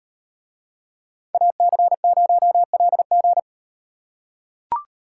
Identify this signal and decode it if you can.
Morse code, heard as a keyed tone, about 38 wpm, 700 Hz. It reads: AC0LG